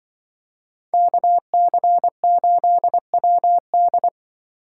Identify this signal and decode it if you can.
Morse code, heard as a keyed tone, about 24 wpm, 700 Hz. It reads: KC8WD
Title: Speed Racing - USA 2x2 Call Signs 20wpm